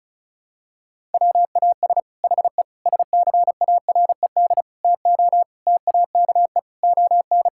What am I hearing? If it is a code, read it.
Morse code, heard as a keyed tone, about 34 wpm, 700 Hz. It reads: WAS HE SCARED TO TAKE ON
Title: Sentences from Top 1000 Words - Review Part 6 - 35wpm